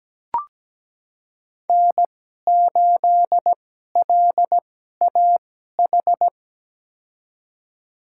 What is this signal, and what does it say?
Morse code, heard as a keyed tone, about 17 wpm, spaced as for 12 wpm, 700 Hz. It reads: N8LAH
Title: USA Call Signs - 5 Characters - 2x Character Spacing - 17wpm